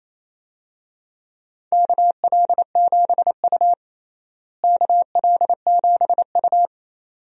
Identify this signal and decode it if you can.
Morse code, heard as a keyed tone, about 28 wpm, 700 Hz. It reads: KL7U KL7U
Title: DXCC-100 Call Signs - 4 Characters - Repeated 2x - 28wpm